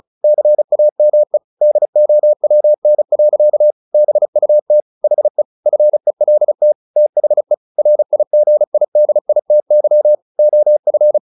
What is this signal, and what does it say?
Morse code, heard as a keyed tone, about 34 wpm, 600 Hz. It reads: CAME DOWN. BUT HE FELT THE RIGIDITY OF
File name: the old man and the sea 2 35wpm